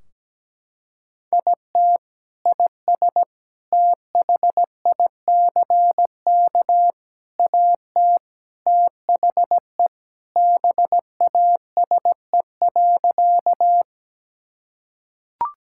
Morse code, heard as a keyed tone, about 17 wpm, 700 Hz. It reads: IT IS THICK AT THE BASE.
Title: Sentences from Top 1600 Words 17wpm